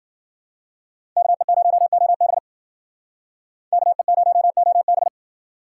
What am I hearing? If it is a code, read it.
Morse code, heard as a keyed tone, about 50 wpm, 700 Hz. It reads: KE0OB KE0OB
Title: Speed Racing - USA 2x2 Call Signs 40wpm